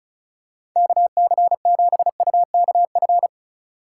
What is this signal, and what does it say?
Morse code, heard as a keyed tone, about 36 wpm, 700 Hz. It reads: KC7UKF
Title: USA Call Signs - 6 Characters - 4x Character Spacing - 35wpm